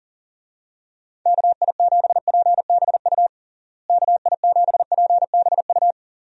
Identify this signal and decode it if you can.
Morse code, heard as a keyed tone, about 40 wpm, 700 Hz. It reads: KI7PBU KI7PBU
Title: USA Call Signs - 6 Characters - Repeated 3x - 40wpm